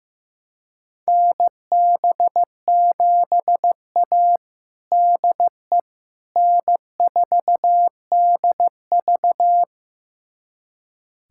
Morse code, heard as a keyed tone, about 15 wpm, 700 Hz. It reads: NB7A DE N4DV
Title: USA Callsign DE Callsign - 4 Characters - 15wpm